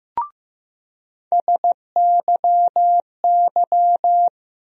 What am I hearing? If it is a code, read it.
Morse code, heard as a keyed tone, about 15 wpm, 700 Hz. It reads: SYY
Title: USA Call Sign Postfixes - 3 Letters 15wpm